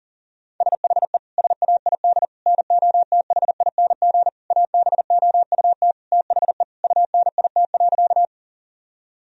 Morse code, heard as a keyed tone, about 40 wpm, 700 Hz. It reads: SHE SAID NOTHING ABOUT THE UNIT.
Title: Sentences from Top 1000 Words - Review Part 2 - 40wpm